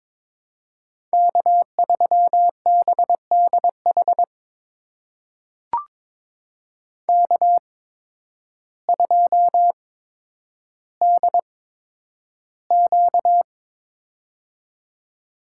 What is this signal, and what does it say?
Morse code, heard as a keyed tone, about 22 wpm, spaced as for 10 wpm, 700 Hz. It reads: K3BDH K2DQ
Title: USA Call Signs - 8x Character Spacing - 22wpm